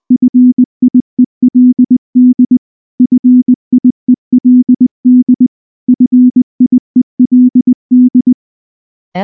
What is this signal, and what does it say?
Morse code, heard as a keyed tone, about 19 wpm, 250 Hz. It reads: FIELD FIELD FIELD
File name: morse / morse-training-20WPM-8